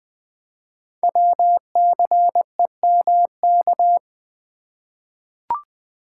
Morse code, heard as a keyed tone, about 20 wpm, 700 Hz. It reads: WCEMK
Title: Code Groups - 5 Letters - 20wpm